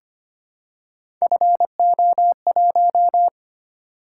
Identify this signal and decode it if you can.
Morse code, heard as a keyed tone, about 25 wpm, 700 Hz. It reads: FO1
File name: Three Letter-Number Combinations - 25wpm